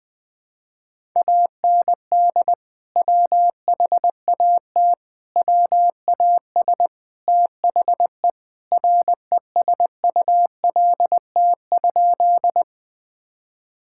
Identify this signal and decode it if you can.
Morse code, heard as a keyed tone, about 20 wpm, 700 Hz. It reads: AND WHAT WAS THE RESULT?
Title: Sentences from Top 1000 Words - Review Part 5 - 20wpm